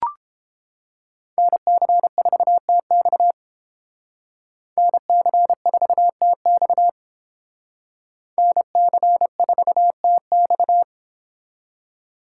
Morse code, heard as a keyed tone, about 31 wpm, 700 Hz. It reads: NC4TX NC4TX NC4TX